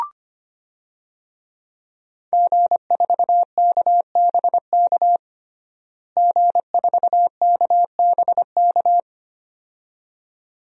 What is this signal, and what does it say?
Morse code, heard as a keyed tone, about 25 wpm, 700 Hz. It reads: G4KBK G4KBK
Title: DXCC-100 Call Signs - 5 Characters - Repeated 2x - 25wpm